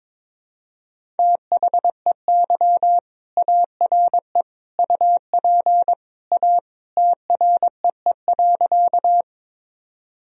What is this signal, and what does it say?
Morse code, heard as a keyed tone, about 22 wpm, 700 Hz. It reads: THEY ARE UP A TREE.
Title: Sentences from Top 1000 Words - Review Part 1 - 22wpm